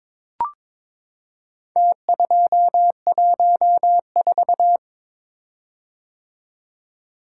Morse code, heard as a keyed tone, about 22 wpm, 700 Hz. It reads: T214
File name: practice / 22wpm / Four Letter-Number Combinations - 22wpm